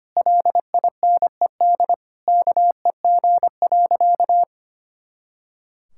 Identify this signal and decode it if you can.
Morse code, heard as a keyed tone, about 25 wpm, 700 Hz. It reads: LINED KEG.
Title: Pangram Challenge v2 - 25wpm